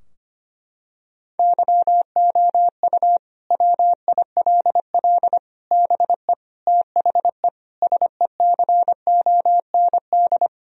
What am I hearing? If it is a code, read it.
Morse code, heard as a keyed tone, about 25 wpm, 700 Hz. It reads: YOU WILL BE THE SECOND
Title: Sentences from Top 1000 Words - Review Part 4 - 25wpm